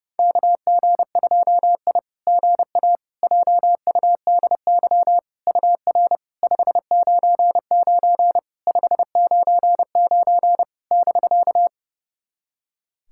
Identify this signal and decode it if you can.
Morse code, heard as a keyed tone, about 30 wpm, 700 Hz. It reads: KG2I GA JUDY UR 599 599 <BK>
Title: SOTA QSO Practice 30wpm